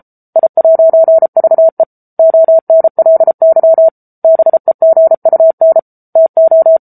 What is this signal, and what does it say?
Morse code, heard as a keyed tone, about 34 wpm, 650 Hz. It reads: I'VE ONLY BEGUN TO
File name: Book Two / 07-Chapter 7 - 33 WPM